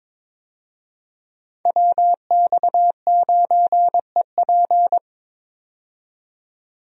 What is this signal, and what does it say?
Morse code, heard as a keyed tone, about 22 wpm, 700 Hz. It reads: WX9EP
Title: USA Call Signs - 5 Characters - 22wpm